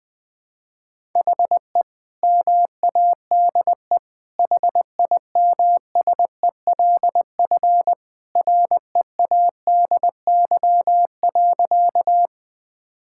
Morse code, heard as a keyed tone, about 20 wpm, 700 Hz. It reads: HE MADE HIMSELF READY.
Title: Sentences from Top 1600 Words 20wpm